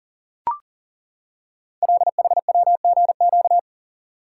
Morse code, heard as a keyed tone, about 39 wpm, 700 Hz. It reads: LHWGQ
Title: Code Groups - 5 Letters - 40wpm